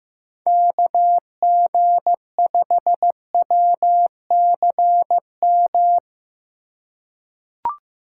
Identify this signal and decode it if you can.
Morse code, heard as a keyed tone, about 15 wpm, 700 Hz. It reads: KG5WCM
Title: USA Call Signs - 2x Character Spacing - 15wpm